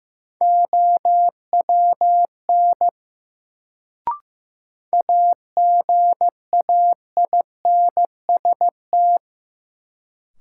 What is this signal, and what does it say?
Morse code, heard as a keyed tone, about 15 wpm, 700 Hz. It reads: OWN AGAINST
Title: Sets of 2 Words - New Word Drill - 15wpm